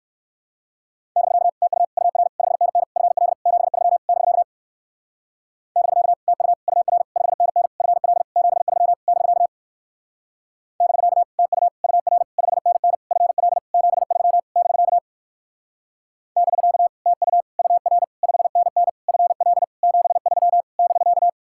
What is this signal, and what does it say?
Morse code, heard as a keyed tone, about 59 wpm, 700 Hz. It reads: <BK> TU UR 5NN FL 73 <BK> <BK> TU UR 5NN FL 73 <BK> <BK> TU UR 5NN FL 73 <BK> <BK> TU UR 5NN FL 73 <BK>